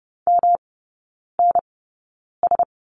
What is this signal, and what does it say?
Morse code, heard as a keyed tone, about 30 wpm, spaced as for 9 wpm, 700 Hz. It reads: MNS
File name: Lesson 18e - Letter m - Three Characters 30wpm